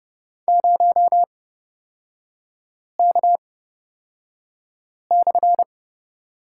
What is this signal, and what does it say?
Morse code, heard as a keyed tone, about 30 wpm, spaced as for 5 wpm, 700 Hz. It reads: 0K/
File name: Lesson 37e - Letter x - Three Characters 30wpm